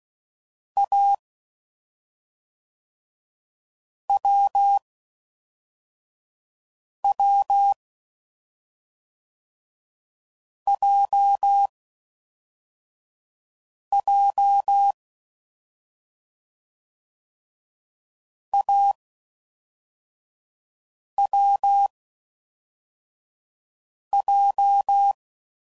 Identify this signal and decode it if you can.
Morse code, heard as a keyed tone, about 16 wpm, 800 Hz. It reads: A W W J J A W J